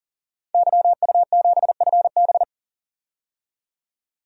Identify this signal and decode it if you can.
Morse code, heard as a keyed tone, about 40 wpm, 700 Hz. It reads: YU7FB
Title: DXCC-100 Call Signs - 5 Characters - 40wpm